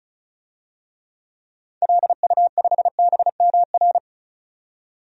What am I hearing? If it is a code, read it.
Morse code, heard as a keyed tone, about 36 wpm, 700 Hz. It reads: LU5BMR
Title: DXCC-100 Call Signs - 6 Characters - 35wpm